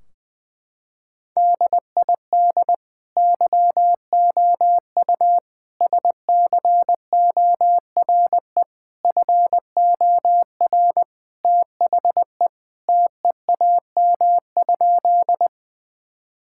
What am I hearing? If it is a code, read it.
Morse code, heard as a keyed tone, about 20 wpm, 700 Hz. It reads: DID YOU SCORE FOR THE TEAM?
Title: Sentences from Top 1600 Words 20wpm